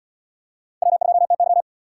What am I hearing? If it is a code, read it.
Morse code, heard as a keyed tone, about 77 wpm, 700 Hz. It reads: W2TY